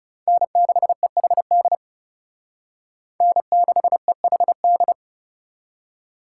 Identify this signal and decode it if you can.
Morse code, heard as a keyed tone, about 37 wpm, 700 Hz. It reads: N6EHD N6EHD
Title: Speed Racing - USA Call Signs 30wpm